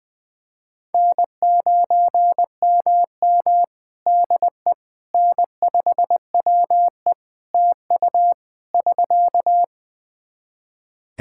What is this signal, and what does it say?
Morse code, heard as a keyed tone, about 21 wpm, 700 Hz. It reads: N9MM DE N5WE TU <SK>
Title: Practice QSO - Exchange 4 - Ending - 20wpm